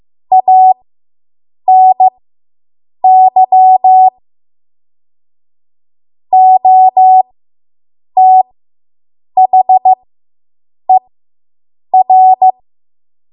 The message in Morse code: ANY OTHER

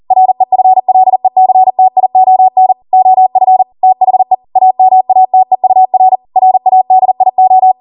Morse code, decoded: REFLECTION OF THE AMATEUR RADIO

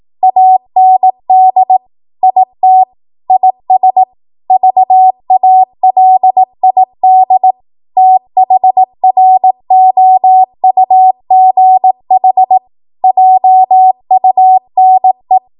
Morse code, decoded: AND IT IS VALID THROUGH JUNE

